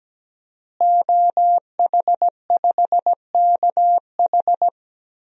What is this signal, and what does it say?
Morse code, heard as a keyed tone, about 17 wpm, 700 Hz. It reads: OH5KH
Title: DXCC-100 Call Signs - 5 Characters - Repeated 2x - 17wpm